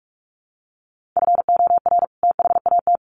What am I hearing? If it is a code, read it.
Morse code, heard as a keyed tone, about 44 wpm, 700 Hz. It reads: FOR THAT
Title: Sentences from Top 1500 Words 45wpm